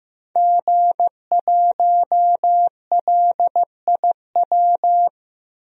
Morse code, heard as a keyed tone, about 15 wpm, 700 Hz. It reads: G1LIW
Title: DXCC-100 Call Signs - 5 Characters - 15wpm